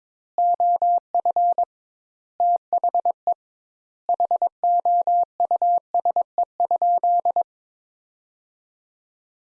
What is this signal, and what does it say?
Morse code, heard as a keyed tone, about 22 wpm, 700 Hz. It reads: OF THE HOUSE?
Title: Sentences Top 200 Words - 22wpm